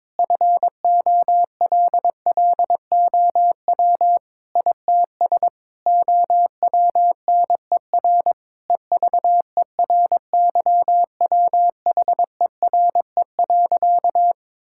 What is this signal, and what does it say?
Morse code, heard as a keyed tone, about 21 wpm, 700 Hz. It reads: FOLLOW ITS OWNER EVERYWHERE.
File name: Difficult Sentences with Hints - 22wpm